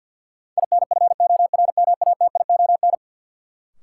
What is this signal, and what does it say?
Morse code, heard as a keyed tone, about 52 wpm, 700 Hz. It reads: INFORMATION
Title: Sentences from Top 600 Words - New Word Drill - 50wpm